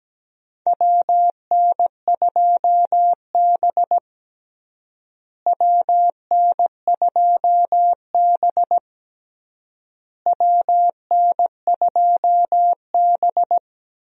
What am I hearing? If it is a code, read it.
Morse code, heard as a keyed tone, about 17 wpm, 700 Hz. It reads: WN2B WN2B WN2B
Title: USA Call Signs - 4 Characters - Repeated 3x - 17wpm